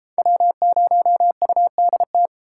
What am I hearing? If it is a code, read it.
Morse code, heard as a keyed tone, about 33 wpm, 700 Hz. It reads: W0UDT